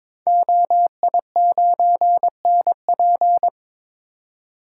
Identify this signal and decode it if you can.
Morse code, heard as a keyed tone, about 22 wpm, 700 Hz. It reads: OI9NP
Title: Code Groups - 5 Letter:Numbers - 22wpm